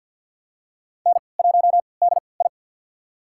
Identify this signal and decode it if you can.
Morse code, heard as a keyed tone, about 45 wpm, spaced as for 27 wpm, 700 Hz. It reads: N1DI
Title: USA Call Signs - 4 Characters - 3x Character Spacing - 50wpm